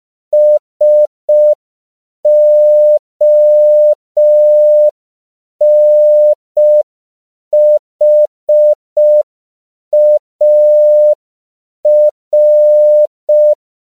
Morse code, heard as a keyed tone, about 5 wpm, 600 Hz. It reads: SONHAR
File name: CW / 5PPM